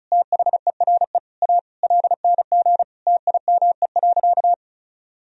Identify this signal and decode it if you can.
Morse code, heard as a keyed tone, about 36 wpm, 700 Hz. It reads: THERE A LNG TIME.
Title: Sentences from Top 400 Words and Abbreviations 35wpm